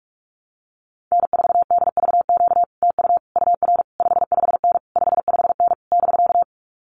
Morse code, heard as a keyed tone, about 46 wpm, 700 Hz. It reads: N4DVQ TU UR 55N 55N <BK>
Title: Common QSO Phrases Encore 45wpm